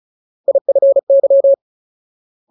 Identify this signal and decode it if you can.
Morse code, heard as a keyed tone, about 34 wpm, 550 Hz. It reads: IFY
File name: Ultimate Word List / 35WPM Ultimate